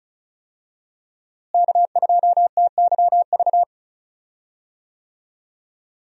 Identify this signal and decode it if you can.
Morse code, heard as a keyed tone, about 34 wpm, 700 Hz. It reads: K2TYV
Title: USA Call Signs - 5 Characters - Repeated 2x - 35wpm